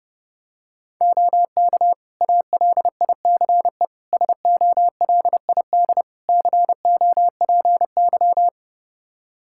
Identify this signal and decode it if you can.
Morse code, heard as a keyed tone, about 30 wpm, 700 Hz. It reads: OK ALICE SOLID COPY
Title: Practice QSO - Exchange 3 - More Info - 30wpm